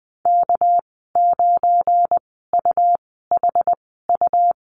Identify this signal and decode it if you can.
Morse code, heard as a keyed tone, about 20 wpm, spaced as for 14 wpm, 700 Hz. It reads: K9UHU
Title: 219 Lesson 35g - Number 0 - Call Signs 20wpm